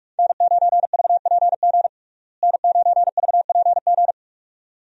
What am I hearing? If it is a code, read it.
Morse code, heard as a keyed tone, about 46 wpm, 700 Hz. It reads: N9VPG N9VPG
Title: USA Call Signs - 5 Characters - Repeated 2x - 45wpm